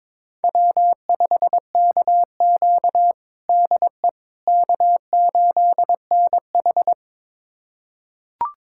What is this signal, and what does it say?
Morse code, heard as a keyed tone, about 22 wpm, 700 Hz. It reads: W5KQ DE K8NH
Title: USA Callsign DE Callsign - 4 Characters - 22wpm